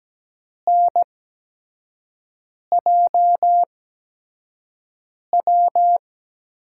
Morse code, heard as a keyed tone, about 17 wpm, spaced as for 5 wpm, 700 Hz. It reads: NJW
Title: USA Call Signs - 6 Characters - 8x Character Spacing - 17wpm